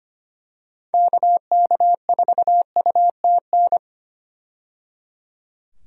Morse code, heard as a keyed tone, about 25 wpm, 700 Hz. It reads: KK4UTN
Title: USA Call Signs - 6 Characters - 25wpm